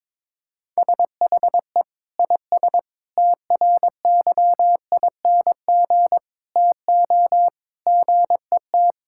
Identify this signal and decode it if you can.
Morse code, heard as a keyed tone, about 22 wpm, 700 Hz. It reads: SHE IS TRYING TO GET
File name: practice / 22wpm / Sentences with 3-Word Phrases - 22wpm